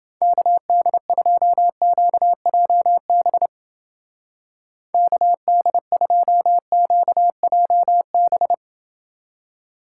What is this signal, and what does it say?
Morse code, heard as a keyed tone, about 28 wpm, 700 Hz. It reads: KD2QJB KD2QJB